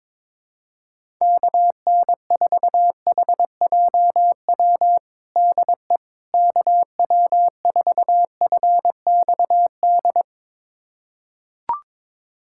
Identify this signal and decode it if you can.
Morse code, heard as a keyed tone, about 22 wpm, 700 Hz. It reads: KN4HJW DE KW4FXD
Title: USA Callsign DE Callsign - 6 Characters - 22wpm